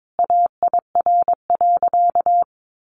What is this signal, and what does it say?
Morse code, heard as a keyed tone, about 22 wpm, 700 Hz. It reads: AIR.
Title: Sentences from Top 900 Words 22wpm